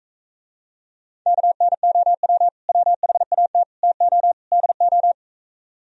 Morse code, heard as a keyed tone, about 45 wpm, 700 Hz. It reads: KNOW WHAT TO DO